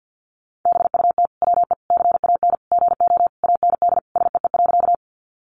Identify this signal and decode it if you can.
Morse code, heard as a keyed tone, about 50 wpm, 700 Hz. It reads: BUT WE CAN GO AND SEE.
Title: Sentences from Top 100 Words 50wpm